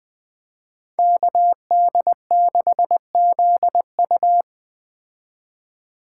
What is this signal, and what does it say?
Morse code, heard as a keyed tone, about 20 wpm, 700 Hz. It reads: KD6ZU